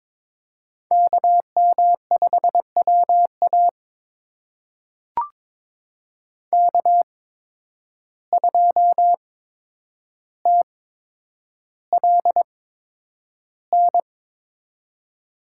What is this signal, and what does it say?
Morse code, heard as a keyed tone, about 23 wpm, spaced as for 10 wpm, 700 Hz. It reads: KM5WA K2TLN
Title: USA Call Signs - 8x Character Spacing - 22wpm